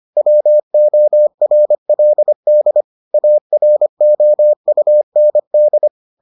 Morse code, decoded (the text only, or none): WORLD AROUND